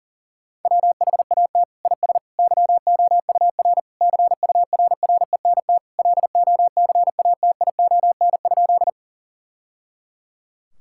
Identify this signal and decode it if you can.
Morse code, heard as a keyed tone, about 37 wpm, 700 Hz. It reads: WHAT IS YOUR CURRENT LOCATION?
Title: Sentences from Top 1000 Words - Review Part 6 - 40wpm